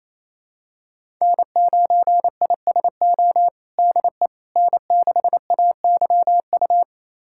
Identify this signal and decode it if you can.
Morse code, heard as a keyed tone, about 28 wpm, 700 Hz. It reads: N9ISO DE N6AYU
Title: USA Callsign DE Callsign - 5 Characters - 28wpm